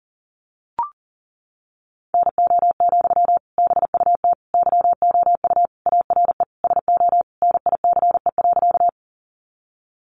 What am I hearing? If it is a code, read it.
Morse code, heard as a keyed tone, about 40 wpm, 700 Hz. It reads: NO, BUT YOU ARE SO NICE.
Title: Sentences from Top 1100 Words 40wpm